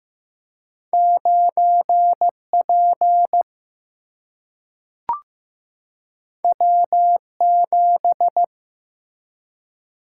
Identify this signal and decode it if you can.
Morse code, heard as a keyed tone, about 15 wpm, 700 Hz. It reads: 9 P W 7
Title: Single-Letter Single-Number 15wpm